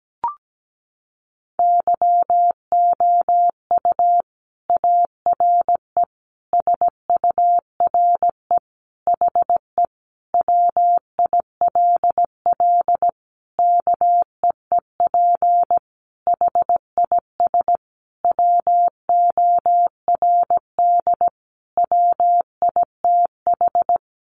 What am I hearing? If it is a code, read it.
Morse code, heard as a keyed tone, about 17 wpm, 700 Hz. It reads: YOU ARE SURE HE WILL KEEP HIS WORD WITH
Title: Sentences from Top 300 Words 17wpm